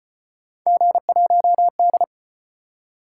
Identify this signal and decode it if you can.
Morse code, heard as a keyed tone, about 33 wpm, 700 Hz. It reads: G1D